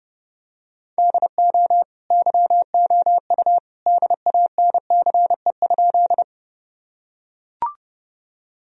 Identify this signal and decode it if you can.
Morse code, heard as a keyed tone, about 30 wpm, 700 Hz. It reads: DO YOU DANCE?
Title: Sentences from Top 1000 Words - Review Part 4 - 30wpm